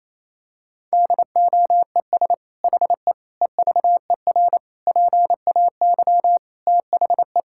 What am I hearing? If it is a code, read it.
Morse code, heard as a keyed tone, about 28 wpm, 700 Hz. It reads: DOES HE EVER PAY THE
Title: Sentences from Top 1000 Words - Review Part 5 - 28wpm